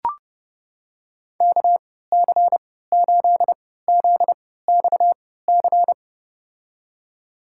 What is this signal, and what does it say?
Morse code, heard as a keyed tone, about 32 wpm, spaced as for 17 wpm, 700 Hz. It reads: KC8ZXC
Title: USA Call Signs - 6 Characters - 3x Character Spacing - 30wpm